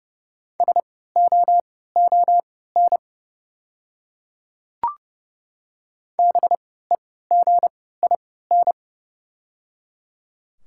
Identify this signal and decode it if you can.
Morse code, heard as a keyed tone, about 31 wpm, spaced as for 17 wpm, 700 Hz. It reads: SOON BEGIN
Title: 3x Character Spacing - Top 500 Words - 30wpm